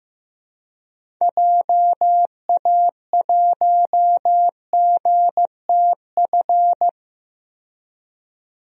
Morse code, heard as a keyed tone, about 15 wpm, 700 Hz. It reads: JA1GTF
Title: DXCC-100 Call Signs - 6 Characters - 15wpm